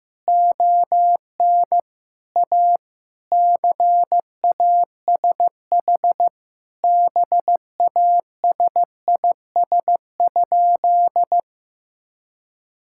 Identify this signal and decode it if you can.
Morse code, heard as a keyed tone, about 15 wpm, 700 Hz. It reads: ON A CASH BASIS?
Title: Sentences from Top 1000 Words - Review Part 5 - 15wpm